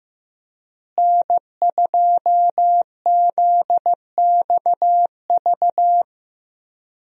Morse code, heard as a keyed tone, about 15 wpm, 700 Hz. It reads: N2ZXV
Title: USA Call Signs - 5 Characters - Repeated 3x - 15wpm